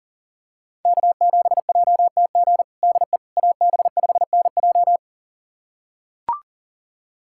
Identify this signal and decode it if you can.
Morse code, heard as a keyed tone, about 40 wpm, 700 Hz. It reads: K7JTG DE AB5NJ
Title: USA Callsign DE Callsign - 5 Characters - 40wpm